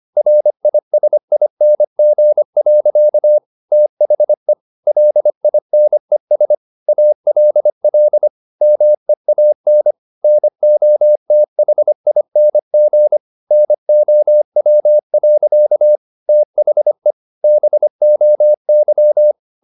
Morse code, heard as a keyed tone, about 25 wpm, 600 Hz. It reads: RISING. THE LINES ALL MEAN NOTHING NOW. THE BOY